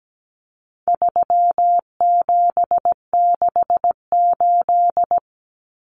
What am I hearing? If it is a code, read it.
Morse code, heard as a keyed tone, about 17 wpm, 700 Hz. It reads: 3768
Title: Numbers - 4-Digit 17wpm